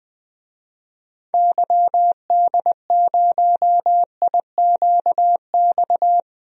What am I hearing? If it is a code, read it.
Morse code, heard as a keyed tone, about 20 wpm, 700 Hz. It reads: YD0IQX
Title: DXCC-100 Call Signs - 6 Characters - 20wpm